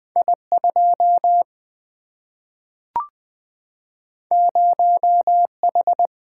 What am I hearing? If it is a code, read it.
Morse code, heard as a keyed tone, about 20 wpm, 700 Hz. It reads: I2 0H
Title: Single-Letter Single-Number 20wpm